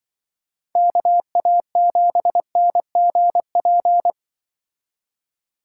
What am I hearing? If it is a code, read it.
Morse code, heard as a keyed tone, about 23 wpm, 700 Hz. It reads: KA7NGP
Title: Speed Racing - USA 2x3 Call Signs 20wpm